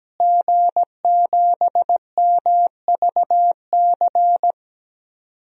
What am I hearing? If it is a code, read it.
Morse code, heard as a keyed tone, about 17 wpm, 700 Hz. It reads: G7MVC
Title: DXCC-100 Call Signs - 5 Characters - Repeated 3x - 17wpm